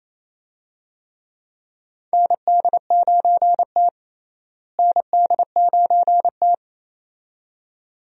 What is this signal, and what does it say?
Morse code, heard as a keyed tone, about 27 wpm, 700 Hz. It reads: ND9T ND9T
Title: USA Call Signs - 4 Characters - Repeated 2x - 28wpm